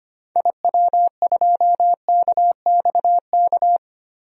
Morse code, heard as a keyed tone, about 25 wpm, 700 Hz. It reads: IW2KXK
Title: DXCC-100 Call Signs - 6 Characters - 25wpm